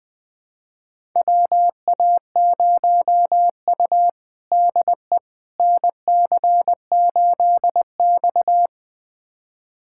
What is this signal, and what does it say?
Morse code, heard as a keyed tone, about 20 wpm, 700 Hz. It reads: WA0U DE NC8X
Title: USA Callsign DE Callsign - 4 Characters - 20wpm